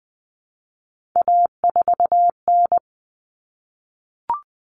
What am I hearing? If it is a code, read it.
Morse code, heard as a keyed tone, about 20 wpm, 700 Hz. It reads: A4N
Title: Common Cut Numbers - 3-Digit 20wpm